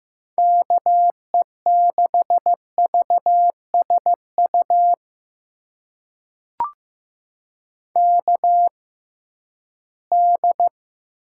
Morse code, heard as a keyed tone, about 15 wpm, spaced as for 11 wpm, 700 Hz. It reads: KE6VSU KD
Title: USA Call Signs - 6 Characters - 6x Character Spacing - 15wpm